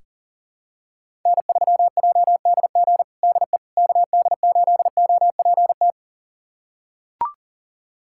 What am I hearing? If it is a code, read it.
Morse code, heard as a keyed tone, about 40 wpm, 700 Hz. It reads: N3JDG DE KD8OPT
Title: USA Callsign DE Callsign 40wpm